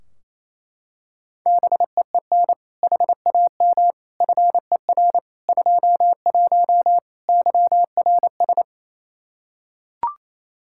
Morse code, heard as a keyed tone, about 28 wpm, 700 Hz. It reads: BEEN HAM FER 21 YRS
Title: Practice QSO - Exchange 3 - More Info - 28wpm